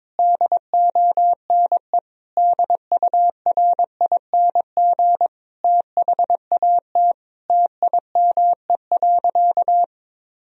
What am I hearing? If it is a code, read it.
Morse code, heard as a keyed tone, about 22 wpm, 700 Hz. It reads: DONE DURING THAT TIME.
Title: Sentences from Top 1000 Words - Review Part 2 - 22wpm